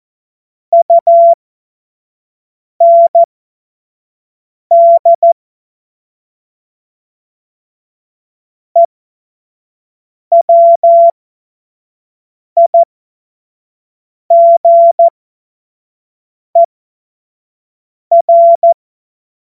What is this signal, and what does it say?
Morse code, heard as a keyed tone, about 14 wpm, spaced as for 5 wpm, 680 Hz. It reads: UND EWIGER
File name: Morsekurs / Witze / Witz 21. Dezember 2021, 50 F70